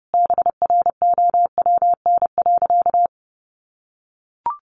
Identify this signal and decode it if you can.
Morse code, heard as a keyed tone, about 30 wpm, 700 Hz. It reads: BROWN.
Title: Sentences from Top 1200 Words 30wpm